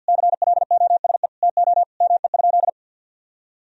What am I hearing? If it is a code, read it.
Morse code, heard as a keyed tone, about 47 wpm, 700 Hz. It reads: CLOSE TO ME?
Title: Sentences from Top 1000 Words - Review Part 1 - 50wpm